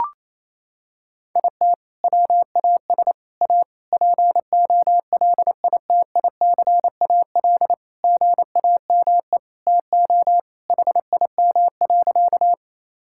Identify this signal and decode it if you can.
Morse code, heard as a keyed tone, about 27 wpm, 700 Hz. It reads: IT WAS A POLITICAL GAME TO HIM.
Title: Sentences from Top 1000 Words - Review Part 5 - 28wpm